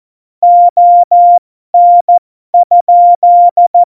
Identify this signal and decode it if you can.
Morse code, heard as a keyed tone, about 14 wpm, spaced as for 12 wpm, 700 Hz. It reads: ON?
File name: Morsekurs / Witze